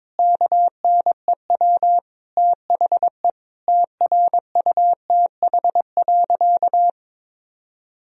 Morse code, heard as a keyed tone, about 22 wpm, 700 Hz. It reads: KNEW THE TRUTH.
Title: Sentences from Top 1000 Words - Review Part 1 - 22wpm